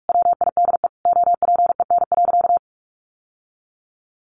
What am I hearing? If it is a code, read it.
Morse code, heard as a keyed tone, about 44 wpm, 700 Hz. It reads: WIDE OPEN.
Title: Sentences from Top 1500 Words 45wpm